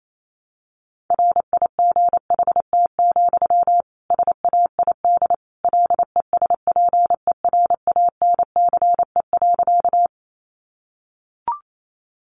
Morse code, heard as a keyed tone, about 28 wpm, 700 Hz. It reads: RIGHT, SAID LESPERANCE.
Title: A Sound Of Thunder 28wpm